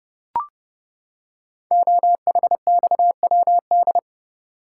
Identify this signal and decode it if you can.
Morse code, heard as a keyed tone, about 30 wpm, 700 Hz. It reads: OHXWD
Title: Code Groups - 5 Letters - 30wpm